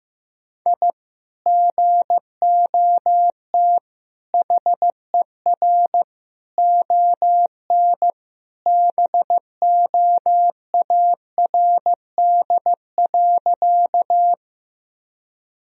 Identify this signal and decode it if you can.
Morse code, heard as a keyed tone, about 15 wpm, 700 Hz. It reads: I GOT HER ON BOARD.
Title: Sentences from Top 1000 Words - Review Part 4 - 15wpm